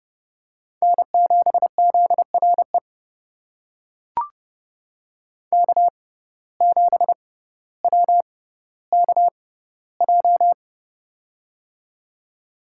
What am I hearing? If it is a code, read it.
Morse code, heard as a keyed tone, about 30 wpm, spaced as for 15 wpm, 700 Hz. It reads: N7ZRE K7WKJ
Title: USA Call Signs - 5 Characters - 6x Character Spacing - 30wpm